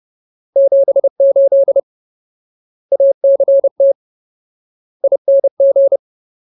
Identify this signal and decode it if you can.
Morse code, heard as a keyed tone, about 30 wpm, 550 Hz. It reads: 78 ACT ING
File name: Ultimate Word List / 30WPM Ultimate